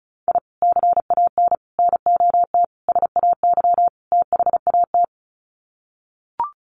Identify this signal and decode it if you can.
Morse code, heard as a keyed tone, about 34 wpm, 700 Hz. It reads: I CAN NOT SAY THAT
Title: Sets of 5 Words Encore2 35wpm